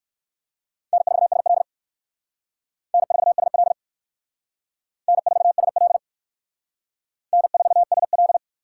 Morse code, heard as a keyed tone, about 56 wpm, 700 Hz. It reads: N4SL N4SL N4SL N4SL
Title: Speed Racing - USA 1x2 Call Signs 45wpm